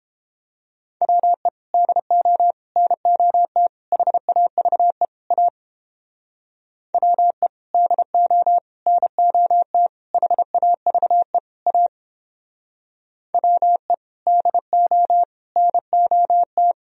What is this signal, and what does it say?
Morse code, heard as a keyed tone, about 29 wpm, 700 Hz. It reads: WE DO NOT HAVE A WE DO NOT HAVE A WE DO NOT